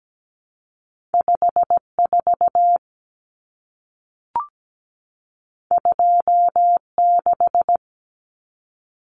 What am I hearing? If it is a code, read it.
Morse code, heard as a keyed tone, about 17 wpm, 700 Hz. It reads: 54 26
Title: Numbers - 2-Digit 17wpm